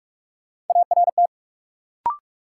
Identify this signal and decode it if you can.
Morse code, heard as a keyed tone, about 46 wpm, 700 Hz. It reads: ART
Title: Sentences from Top 600 Words - New Word Drill - 45wpm